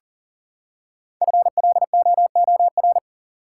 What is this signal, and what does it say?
Morse code, heard as a keyed tone, about 40 wpm, 700 Hz. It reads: FLOOR
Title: Five Letter Words - 40wpm